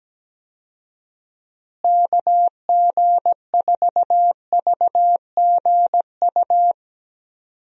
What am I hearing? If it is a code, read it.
Morse code, heard as a keyed tone, about 17 wpm, 700 Hz. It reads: KG4VGU
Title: USA Call Signs - 6 Characters - Repeated 2x - 17wpm